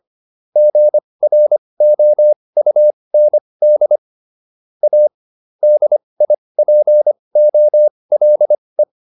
Morse code, heard as a keyed tone, about 25 wpm, spaced as for 20 wpm, 600 Hz. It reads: GROUND A DIPOLE